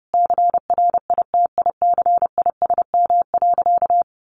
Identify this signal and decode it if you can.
Morse code, heard as a keyed tone, about 30 wpm, 700 Hz. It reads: CRITICISM.
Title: Sentences from Top 1500 Words 30wpm